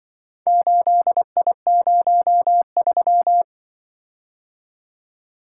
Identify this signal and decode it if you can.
Morse code, heard as a keyed tone, about 25 wpm, 700 Hz. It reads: 8I03
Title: Speed-Racing - Four Letter-Number Combinations - 20wpm